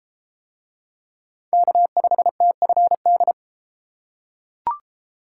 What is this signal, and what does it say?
Morse code, heard as a keyed tone, about 33 wpm, 700 Hz. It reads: K5TFD